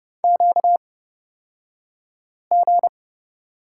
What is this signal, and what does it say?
Morse code, heard as a keyed tone, about 30 wpm, spaced as for 5 wpm, 700 Hz. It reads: QG
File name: course / 30:10wpm / 235 Lesson 38d - Letter q - Two Characters 30wpm